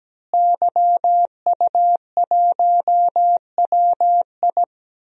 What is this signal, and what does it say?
Morse code, heard as a keyed tone, about 17 wpm, 700 Hz. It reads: YU1WI
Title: DXCC-100 Call Signs - 5 Characters - Repeated 2x - 17wpm